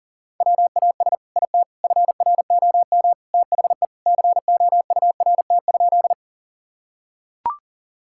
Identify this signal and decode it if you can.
Morse code, heard as a keyed tone, about 40 wpm, 700 Hz. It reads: WAS IT FROM THE COURT?
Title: Sentences from Top 1600 Words 40wpm